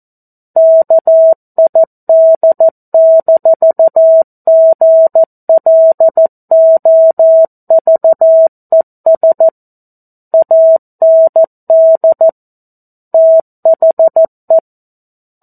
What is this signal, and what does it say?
Morse code, heard as a keyed tone, about 14 wpm, 650 Hz. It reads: KID-GLOVES AND THE
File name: 02-Chapter-02 14 WPM